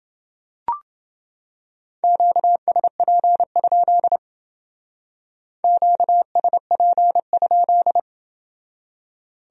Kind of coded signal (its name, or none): Morse code